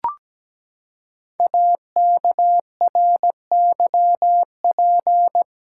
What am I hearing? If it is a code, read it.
Morse code, heard as a keyed tone, about 17 wpm, 700 Hz. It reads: AKRYP
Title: Code Groups - 5 Letters - 17wpm